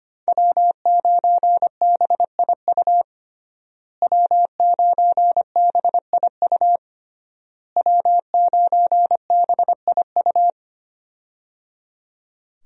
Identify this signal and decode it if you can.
Morse code, heard as a keyed tone, about 25 wpm, 700 Hz. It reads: W9BIU W9BIU W9BIU
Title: USA Call Signs - 5 Characters - Repeated 3x - 25wpm